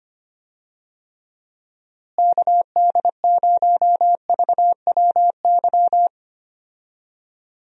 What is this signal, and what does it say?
Morse code, heard as a keyed tone, about 25 wpm, 700 Hz. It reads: KD0VWY
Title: USA Call Signs - 6 Characters - Repeated 2x - 25wpm